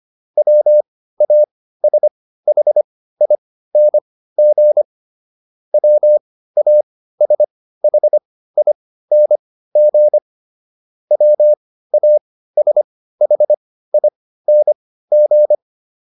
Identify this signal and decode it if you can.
Morse code, heard as a keyed tone, about 25 wpm, spaced as for 15 wpm, 600 Hz. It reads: WASHING WASHING WASHING